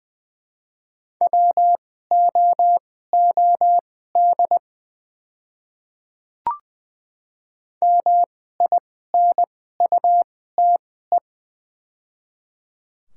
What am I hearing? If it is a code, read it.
Morse code, heard as a keyed tone, about 20 wpm, spaced as for 15 wpm, 700 Hz. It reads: WOOD MINUTE
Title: 2x Character Spacing - Top 500 Words - 20wpm